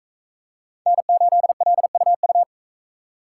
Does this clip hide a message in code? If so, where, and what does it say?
Yes: Morse code from 0.9 to 2.4 s, heard as a keyed tone, about 42 wpm, 700 Hz. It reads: N8LUU